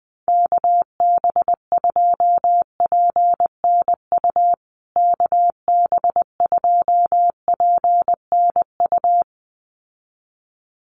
Morse code, heard as a keyed tone, about 19 wpm, 700 Hz. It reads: KB2PNU KB2PNU
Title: USA Call Signs Repeated 2x 20wpm